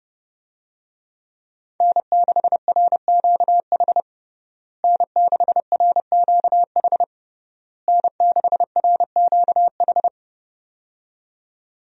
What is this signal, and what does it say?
Morse code, heard as a keyed tone, about 30 wpm, 700 Hz. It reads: N6RQH N6RQH N6RQH
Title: USA Call Signs - 5 Characters - Repeated 3x - 30wpm